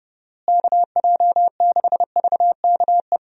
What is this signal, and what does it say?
Morse code, heard as a keyed tone, about 30 wpm, 700 Hz. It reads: KJ6VKE
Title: USA Call Signs - 6 Characters - Repeated 3x - 30wpm